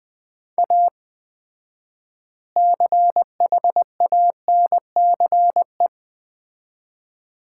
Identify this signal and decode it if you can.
Morse code, heard as a keyed tone, about 21 wpm, 700 Hz. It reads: A CHANCE
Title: 4x Word-Spacing - Sets of 3 Words - 20wpm